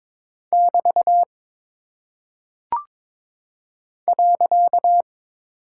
Morse code, heard as a keyed tone, about 22 wpm, 700 Hz. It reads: <BT> .
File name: practice / 22wpm / Single Punctuation-Prosign - 22wpm